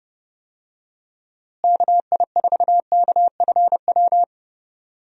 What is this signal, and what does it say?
Morse code, heard as a keyed tone, about 30 wpm, 700 Hz. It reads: KI4KFW
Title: USA Call Signs - 6 Characters - Repeated 3x - 30wpm